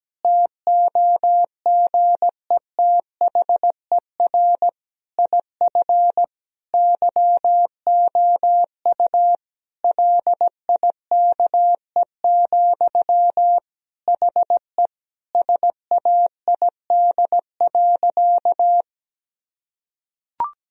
Morse code, heard as a keyed tone, about 17 wpm, 700 Hz. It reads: TOGETHER IF YOU LIKE, HE SAID.
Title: Sentences from Top 1000 Words - Review Part 2 - 17wpm